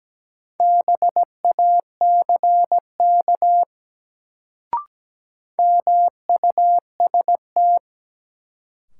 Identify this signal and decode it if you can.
Morse code, heard as a keyed tone, about 17 wpm, 700 Hz. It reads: BACK MUST
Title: Sentences from Top 200 Words - New Word Drill - 17wpm